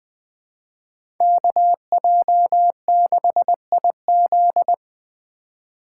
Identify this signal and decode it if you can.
Morse code, heard as a keyed tone, about 20 wpm, 700 Hz. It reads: KJ6IZ
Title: USA Call Signs - 3x Character Spacing - 20wpm